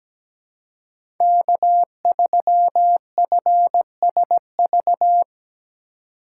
Morse code, heard as a keyed tone, about 17 wpm, 700 Hz. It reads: K3FSV
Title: USA Call Signs - 5 Characters - 4x Character Spacing - 17wpm